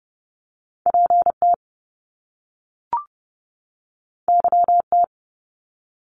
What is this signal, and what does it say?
Morse code, heard as a keyed tone, about 30 wpm, 700 Hz. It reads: P T Y T
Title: Top Two Letter Combinations in Words 30wpm